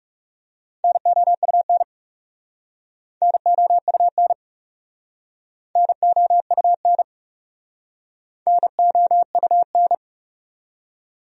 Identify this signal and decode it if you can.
Morse code, heard as a keyed tone, about 41 wpm, 700 Hz. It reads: NOUN NOUN NOUN NOUN